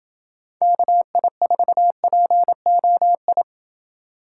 Morse code, heard as a keyed tone, about 26 wpm, 700 Hz. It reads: KI4POI